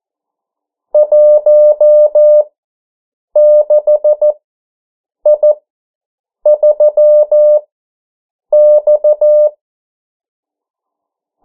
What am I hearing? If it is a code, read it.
Morse code, heard as a keyed tone, about 14 wpm, spaced as for 7 wpm, 600 Hz. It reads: 16I3X
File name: morse mp3 / 09 urok 8